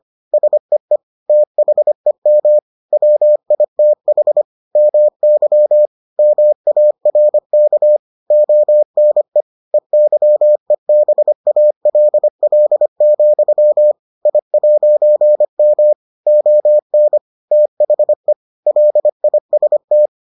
Morse code, heard as a keyed tone, about 25 wpm, 600 Hz. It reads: SEE THEM WITH MY MARK ONE EYEBALL, I'M ON THE LIST